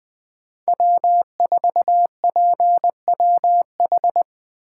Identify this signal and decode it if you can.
Morse code, heard as a keyed tone, about 20 wpm, 700 Hz. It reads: W4PWH